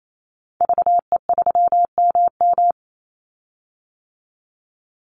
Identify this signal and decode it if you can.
Morse code, heard as a keyed tone, about 28 wpm, 700 Hz. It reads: VE3MM
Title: Call Signs - 28wpm